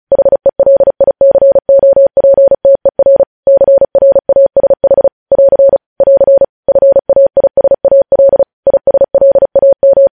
Morse code, heard as a keyed tone, about 36 wpm, 550 Hz. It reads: HELICOPTER CRASH <AR> <AR> FAISAL ISLAM
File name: podcast / News headlines-1x-35x35